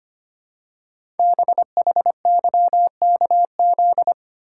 Morse code, heard as a keyed tone, about 25 wpm, 700 Hz. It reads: BHYKZ
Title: Code Groups - 5 Letters - 25wpm